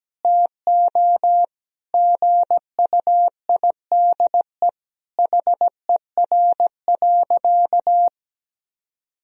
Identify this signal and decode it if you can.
Morse code, heard as a keyed tone, about 17 wpm, 700 Hz. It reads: TO GUIDE HER.
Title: Sentences from Top 1000 Words - Review Part 4 - 17wpm